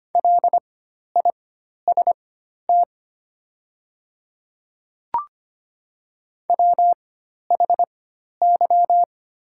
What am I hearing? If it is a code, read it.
Morse code, heard as a keyed tone, about 25 wpm, spaced as for 12 wpm, 700 Hz. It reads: LIST WHY
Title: 4x Character Spacing - Top 500 Words - 25wpm